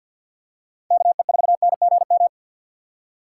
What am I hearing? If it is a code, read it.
Morse code, heard as a keyed tone, about 50 wpm, 700 Hz. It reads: KE4NGM
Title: USA Call Signs - 6 Characters - 6x Character Spacing - 50wpm